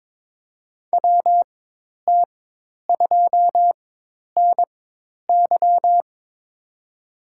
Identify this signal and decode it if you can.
Morse code, heard as a keyed tone, about 21 wpm, spaced as for 10 wpm, 700 Hz. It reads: WT2NY
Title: USA Call Signs - 4x Character Spacing - 22wpm